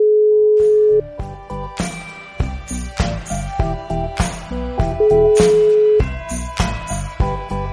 Ringback tone (Europe, 425 Hz).